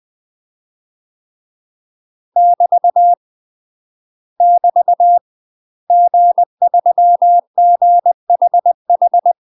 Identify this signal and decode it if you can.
Morse code, heard as a keyed tone, about 20 wpm, 700 Hz. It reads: <BT> <BT> G3GHH
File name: UK MASTER QSOs TRAINING / 20WPM / UK MASTER QSOs 0049